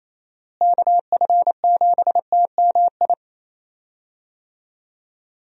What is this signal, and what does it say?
Morse code, heard as a keyed tone, about 28 wpm, 700 Hz. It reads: KF7TMI
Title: USA Call Signs - 6 Characters - Repeated 3x - 28wpm